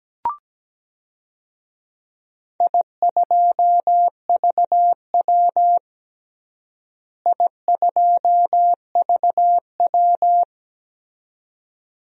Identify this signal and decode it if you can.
Morse code, heard as a keyed tone, about 17 wpm, 700 Hz. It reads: I2VW I2VW
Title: DXCC-100 Call Signs - 4 Characters - Repeated 2x - 17wpm